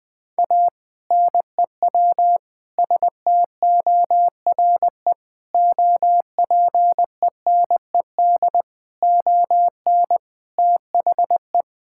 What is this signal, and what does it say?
Morse code, heard as a keyed tone, about 19 wpm, 700 Hz. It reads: A NEW STORE OPENED ON THE
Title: Sentences with 6-Word Phrases - 20wpm